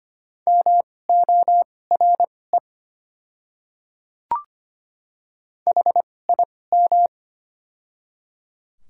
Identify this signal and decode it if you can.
Morse code, heard as a keyed tone, about 25 wpm, spaced as for 18 wpm, 690 Hz. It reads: MORE HIM
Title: 2x Character Spacing - Top 100 Words - 25wpm